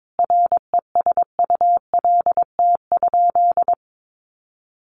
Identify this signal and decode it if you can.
Morse code, heard as a keyed tone, about 22 wpm, 700 Hz. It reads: RESULT?
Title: Sentences from Top 700 Words 22wpm